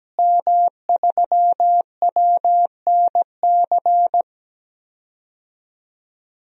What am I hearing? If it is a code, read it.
Morse code, heard as a keyed tone, about 17 wpm, 700 Hz. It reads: M3WNC
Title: DXCC-100 Call Signs - 5 Characters - Repeated 2x - 17wpm